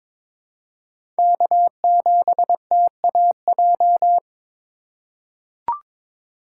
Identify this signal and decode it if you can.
Morse code, heard as a keyed tone, about 22 wpm, 700 Hz. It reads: K7TAJ
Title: USA Call Signs - 5 Characters - 4x Character Spacing - 22wpm